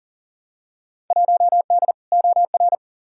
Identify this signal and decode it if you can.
Morse code, heard as a keyed tone, about 40 wpm, 700 Hz. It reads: 1D OR
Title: Field Day Exchanges - 40wpm